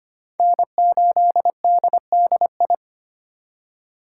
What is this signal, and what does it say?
Morse code, heard as a keyed tone, about 32 wpm, 700 Hz. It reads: N8DDI